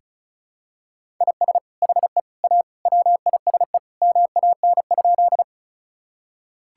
Morse code, heard as a keyed tone, about 36 wpm, 700 Hz. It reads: IS HE A WISE MAN?
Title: Sentences from Top 1600 Words 35wpm